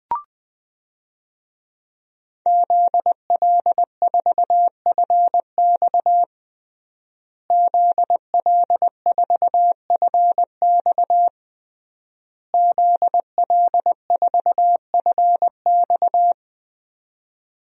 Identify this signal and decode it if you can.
Morse code, heard as a keyed tone, about 20 wpm, 700 Hz. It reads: ZL4FX ZL4FX ZL4FX
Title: DXCC-100 Call Signs - 5 Characters - Repeated 3x - 20wpm